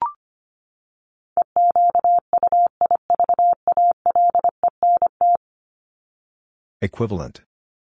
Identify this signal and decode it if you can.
Morse code, heard as a keyed tone, about 25 wpm, 700 Hz. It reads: EQUIVALENT
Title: Top 1501-1750 Words 25wpm